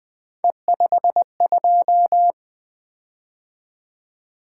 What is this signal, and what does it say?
Morse code, heard as a keyed tone, about 20 wpm, 700 Hz. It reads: E52
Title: Three Letter-Number Combinations - 20wpm